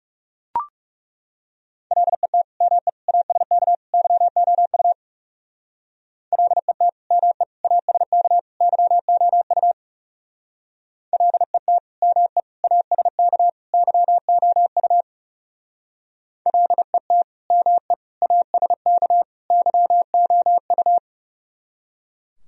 Morse code, heard as a keyed tone, about 36 wpm, 700 Hz. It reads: LET ME ASK YOU LET ME ASK YOU LET ME ASK YOU LET ME ASK YOU
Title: Speed-Racing - Sets of 4 Words 30wpm